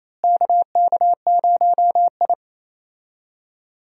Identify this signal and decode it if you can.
Morse code, heard as a keyed tone, about 28 wpm, 700 Hz. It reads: KK0I
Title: USA Call Signs - 4 Characters - Repeated 3x - 28wpm